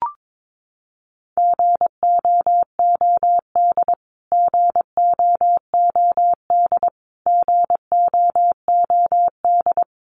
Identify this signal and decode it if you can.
Morse code, heard as a keyed tone, about 22 wpm, 700 Hz. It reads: GOOD GOOD GOOD
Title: Top 200 Words Repeated 22wpm